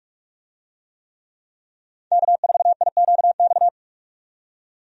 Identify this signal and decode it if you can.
Morse code, heard as a keyed tone, about 44 wpm, 700 Hz. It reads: K4IQX
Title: USA Call Signs - 5 Characters - 45wpm